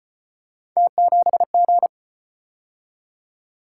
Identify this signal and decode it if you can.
Morse code, heard as a keyed tone, about 34 wpm, 700 Hz. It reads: T7G